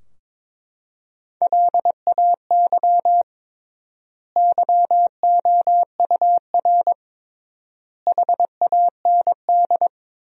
Morse code, heard as a keyed tone, about 22 wpm, 700 Hz. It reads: LAY YOUR HAND
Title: 3x Word-Spacing - Sentences from Top 400 Words - 22wpm